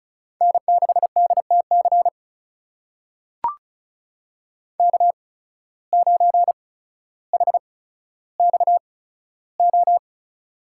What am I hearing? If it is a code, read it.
Morse code, heard as a keyed tone, about 36 wpm, spaced as for 15 wpm, 700 Hz. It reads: N6DTC K9HXO
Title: USA Call Signs - 5 Characters - 8x Character Spacing - 35wpm